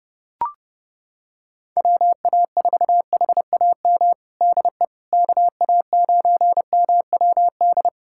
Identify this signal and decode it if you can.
Morse code, heard as a keyed tone, about 30 wpm, 700 Hz. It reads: WA4HAM DE KA9MWD
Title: USA Callsign DE Callsign - 6 Characters - 30wpm